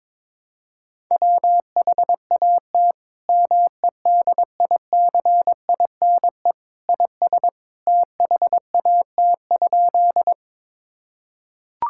Morse code, heard as a keyed tone, about 22 wpm, 700 Hz. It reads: WHAT MEDICINE IS THAT?
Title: Sentences from Top 1000 Words - Review Part 6 - 22wpm